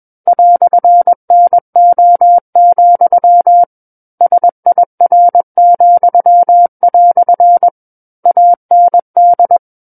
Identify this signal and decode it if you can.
Morse code, heard as a keyed tone, about 21 wpm, 700 Hz. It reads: "NO, SIR," AND
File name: Book One / 12-Chapter 12 - 21 WPM